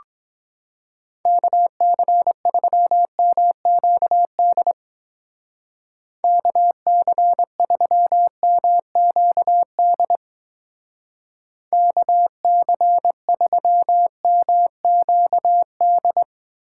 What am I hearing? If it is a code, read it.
Morse code, heard as a keyed tone, about 23 wpm, 700 Hz. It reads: KC3MQD KC3MQD KC3MQD